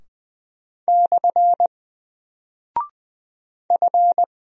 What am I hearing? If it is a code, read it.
Morse code, heard as a keyed tone, about 20 wpm, 700 Hz. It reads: / F
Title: Single Character (Letter:Number:Punctuation:Prosign) Alt 20wpm